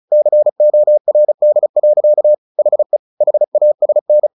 Morse code, heard as a keyed tone, about 34 wpm, 600 Hz. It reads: CORD. HE HASN
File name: the old man and the sea 2 35wpm